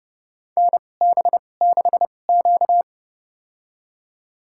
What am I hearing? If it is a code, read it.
Morse code, heard as a keyed tone, about 30 wpm, spaced as for 22 wpm, 700 Hz. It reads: NB6Q
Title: USA Call Signs - 2x Character Spacing - 30wpm